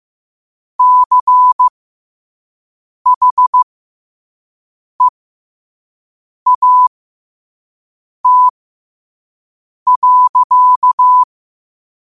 Morse code, heard as a keyed tone, about 15 wpm, spaced as for 5 wpm, 1000 Hz. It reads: CHEAT.
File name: qotdfiles / 5wpm